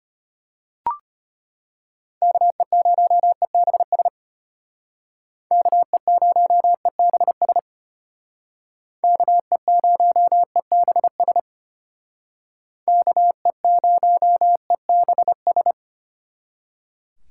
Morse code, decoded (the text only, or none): KE0EBS KE0EBS KE0EBS KE0EBS